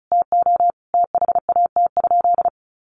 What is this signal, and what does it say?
Morse code, heard as a keyed tone, about 34 wpm, 700 Hz. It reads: TO THAT?
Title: Sentences from Top 1400 Words 35wpm